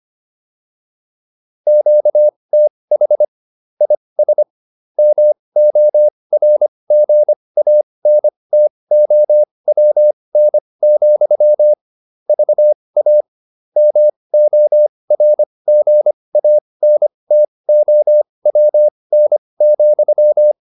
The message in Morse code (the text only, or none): QTH IS MORGANTOWN, VA MORGANTOWN,